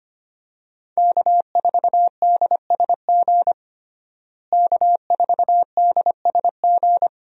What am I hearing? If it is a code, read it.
Morse code, heard as a keyed tone, about 25 wpm, 700 Hz. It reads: K4DSG K4DSG
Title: USA Call Signs - 5 Characters - Repeated 2x - 25wpm